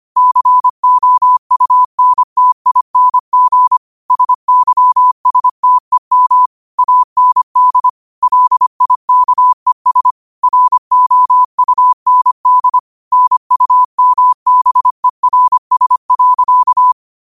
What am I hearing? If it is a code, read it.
Morse code, heard as a keyed tone, about 25 wpm, 1000 Hz. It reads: COUNTING SYSTEM AND LIKES ROUND NUMBERS.